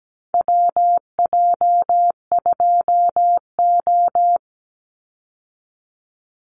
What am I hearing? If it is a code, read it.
Morse code, heard as a keyed tone, about 17 wpm, 700 Hz. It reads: WJ2O
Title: Call Signs - 17wpm